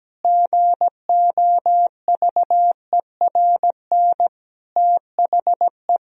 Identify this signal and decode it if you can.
Morse code, heard as a keyed tone, about 17 wpm, 700 Hz. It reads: GOVERN THE
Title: Sentences from Top 1000 Words - Review Part 2 - 17wpm